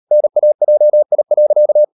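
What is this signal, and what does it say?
Morse code, heard as a keyed tone, about 38 wpm, 600 Hz. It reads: NAJI.
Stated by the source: Morse code practice at 38 WPM, sending Ken Follett's novel Never in English, from CW med Gnister.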